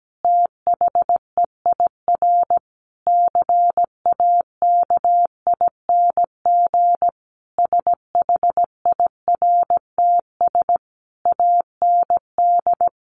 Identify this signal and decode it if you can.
Morse code, heard as a keyed tone, about 17 wpm, 700 Hz. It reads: THEIR CAKING SHIRTS AND
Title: A Sound Of Thunder 17wpm